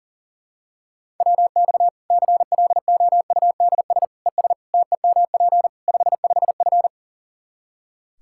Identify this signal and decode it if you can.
Morse code, heard as a keyed tone, about 37 wpm, 700 Hz. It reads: WX CLOUDS ES TEMP 55F
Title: Practice QSO - Exchange 2 - Info - 40wpm